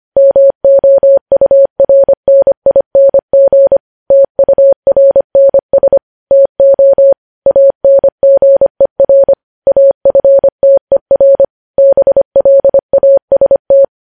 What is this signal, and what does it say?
Morse code, heard as a keyed tone, about 25 wpm, 550 Hz. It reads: MOURNING TURNS TO ANGER AFTER BLAST
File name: podcast / News headlines-1x-25x25